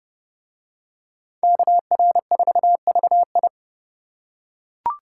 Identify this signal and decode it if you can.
Morse code, heard as a keyed tone, about 28 wpm, 700 Hz. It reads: KR4VI